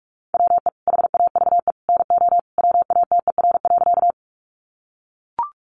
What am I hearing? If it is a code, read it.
Morse code, heard as a keyed tone, about 44 wpm, 700 Hz. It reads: WE HAVE NO WATER.
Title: Sentences from Top 100 Words 45wpm